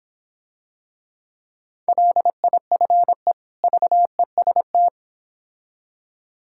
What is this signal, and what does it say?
Morse code, heard as a keyed tone, about 26 wpm, 700 Hz. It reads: LIFE VEST